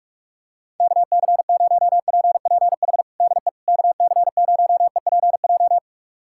Morse code, heard as a keyed tone, about 42 wpm, 700 Hz. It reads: KC0PPH DE KC0EPJ
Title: USA Callsign DE Callsign 45wpm